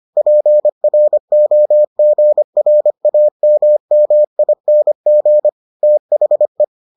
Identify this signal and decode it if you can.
Morse code, heard as a keyed tone, about 25 wpm, 600 Hz. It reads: PROGRAMMING THE